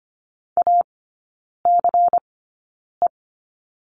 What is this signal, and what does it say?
Morse code, heard as a keyed tone, about 25 wpm, spaced as for 9 wpm, 700 Hz. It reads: ACE
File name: course / 25:10wpm / 100 Lesson 17e - Letter u - Three Characters 25wpm